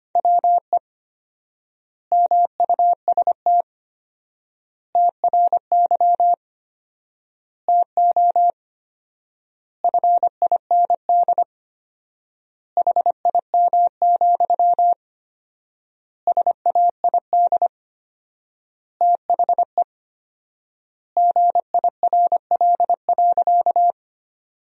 Morse code, heard as a keyed tone, about 25 wpm, 700 Hz. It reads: WE MUST TRY TO FIND HIM, SAID THE GIRL.
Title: 4x Word-Spacing - Sentences from Top 400 Words - 25wpm